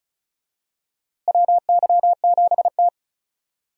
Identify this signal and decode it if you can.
Morse code, heard as a keyed tone, about 36 wpm, 700 Hz. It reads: WY7T
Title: USA Call Signs - 4 Characters - Repeated 2x - 35wpm